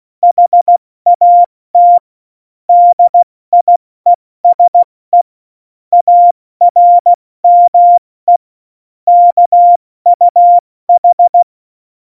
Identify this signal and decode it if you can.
Morse code, heard as a keyed tone, about 16 wpm, 700 Hz. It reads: HAT DIESE ARME KUH
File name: Morsekurs / Witze / Witz 14 Juli 2020, 70 F80